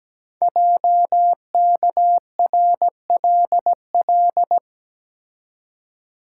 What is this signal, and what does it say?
Morse code, heard as a keyed tone, about 17 wpm, 700 Hz. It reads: JKRLL
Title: Code Groups - 5 Letters - 17wpm